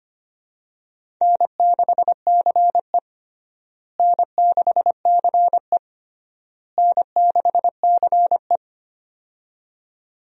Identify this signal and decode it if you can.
Morse code, heard as a keyed tone, about 25 wpm, 700 Hz. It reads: N6CE N6CE N6CE
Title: USA Call Signs - 4 Characters - Repeated 3x - 25wpm